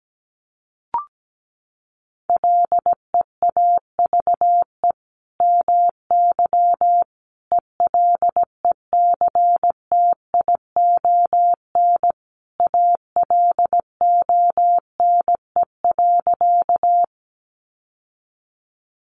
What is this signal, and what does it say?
Morse code, heard as a keyed tone, about 17 wpm, 700 Hz. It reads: LEAVE MY ELECTION ALONE.
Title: Sentences from Top 1500 Words 17wpm